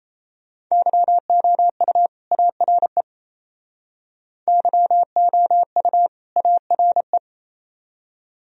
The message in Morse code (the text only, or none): YOU ARE YOU ARE